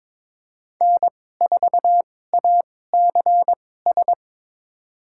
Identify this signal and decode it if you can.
Morse code, heard as a keyed tone, about 22 wpm, spaced as for 16 wpm, 700 Hz. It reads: N4ACS
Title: USA Call Signs - 5 Characters - 2x Character Spacing - 22wpm